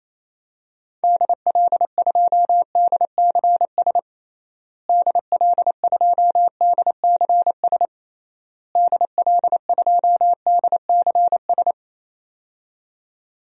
Morse code, heard as a keyed tone, about 27 wpm, 700 Hz. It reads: DL2DCS DL2DCS DL2DCS
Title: DXCC-100 Call Signs - 6 Characters - Repeated 3x - 28wpm